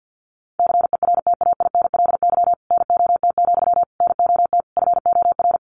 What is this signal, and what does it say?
Morse code, heard as a keyed tone, about 50 wpm, 700 Hz. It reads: CERTAINLY NOT, NOT FOR
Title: Sentences from Top 900 Words 50wpm